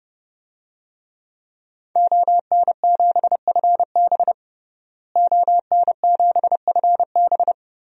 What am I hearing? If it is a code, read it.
Morse code, heard as a keyed tone, about 32 wpm, 700 Hz. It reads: ON7FB ON7FB
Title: DXCC-100 Call Signs - 5 Characters - Repeated 3x - 30wpm